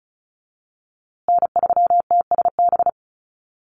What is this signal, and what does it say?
Morse code, heard as a keyed tone, about 34 wpm, 700 Hz. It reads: N3TSB
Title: USA Call Signs Repeated 2x 35wpm